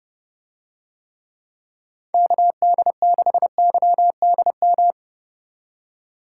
Morse code, heard as a keyed tone, about 27 wpm, 700 Hz. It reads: KD6YDM